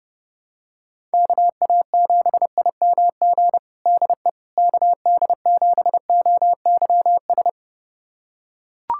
Morse code, heard as a keyed tone, about 30 wpm, 700 Hz. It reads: KA7IMG DE KD7OYS
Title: USA Callsign DE Callsign - 6 Characters - 30wpm